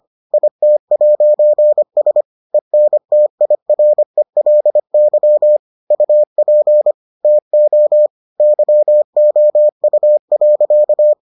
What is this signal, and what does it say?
Morse code, heard as a keyed tone, about 25 wpm, 600 Hz. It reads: IT'S ENTIRELY UP TO YOU.